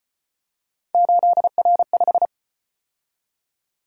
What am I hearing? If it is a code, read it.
Morse code, heard as a keyed tone, about 34 wpm, 700 Hz. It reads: 8R5